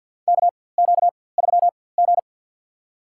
Morse code, heard as a keyed tone, about 53 wpm, spaced as for 23 wpm, 700 Hz. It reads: KQ3G
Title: USA Call Signs - 4x Character Spacing - 50wpm